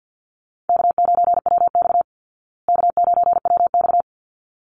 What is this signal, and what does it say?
Morse code, heard as a keyed tone, about 50 wpm, 700 Hz. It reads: K9WX K9WX
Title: Call Signs Repeated 3x - 50wpm